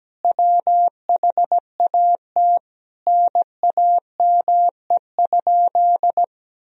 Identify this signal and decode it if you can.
Morse code, heard as a keyed tone, about 17 wpm, 700 Hz. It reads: WHAT NAME?
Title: Sentences from Top 1000 Words - Review Part 1 - 17wpm